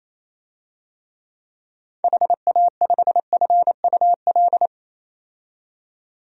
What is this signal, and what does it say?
Morse code, heard as a keyed tone, about 28 wpm, 700 Hz. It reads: HA5FUL
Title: DXCC-100 Call Signs - 6 Characters - 28wpm